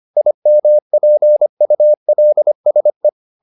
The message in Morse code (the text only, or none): IMPULSE